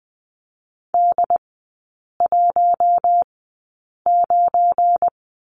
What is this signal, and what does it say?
Morse code, heard as a keyed tone, about 20 wpm, spaced as for 8 wpm, 700 Hz. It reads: D19
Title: 184 Lesson 30e - Letter b - Three Characters 20wpm